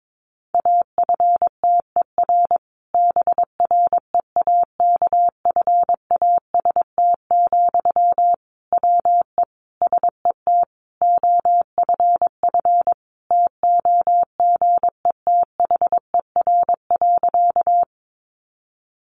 Morse code, heard as a keyed tone, about 22 wpm, 700 Hz. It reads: AFTER BREAKFAST, WE SET OFF TOGETHER.
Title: Sentences from Top 1500 Words 22wpm